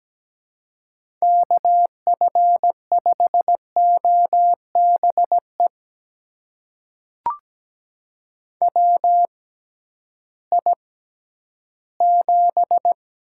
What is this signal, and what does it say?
Morse code, heard as a keyed tone, about 17 wpm, spaced as for 11 wpm, 700 Hz. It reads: KF5OBE WI7
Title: USA Call Signs - 6x Character Spacing - 17wpm